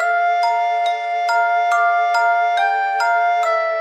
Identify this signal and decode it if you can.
There is no code.